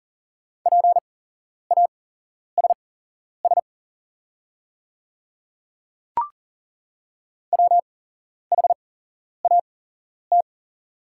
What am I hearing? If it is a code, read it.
Morse code, heard as a keyed tone, about 40 wpm, spaced as for 11 wpm, 690 Hz. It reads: PASS WHAT
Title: 8x Character Spacing - Top 500 Words - 40wpm